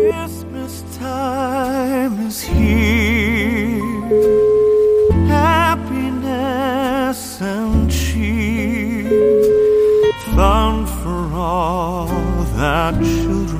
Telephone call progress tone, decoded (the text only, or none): ringback tone